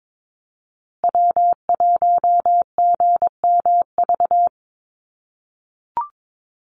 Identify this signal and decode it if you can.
Morse code, heard as a keyed tone, about 22 wpm, 690 Hz. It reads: W1GMV
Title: USA Call Signs Repeated 3x 22wpm